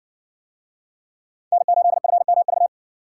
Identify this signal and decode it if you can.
Morse code, heard as a keyed tone, about 55 wpm, 700 Hz. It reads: N8FMV